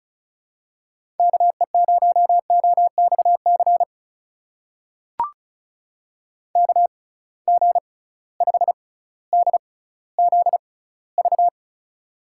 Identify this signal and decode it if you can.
Morse code, heard as a keyed tone, about 36 wpm, spaced as for 18 wpm, 700 Hz. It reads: KE0OXC KG5DZV
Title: USA Call Signs - 6 Characters - 6x Character Spacing - 35wpm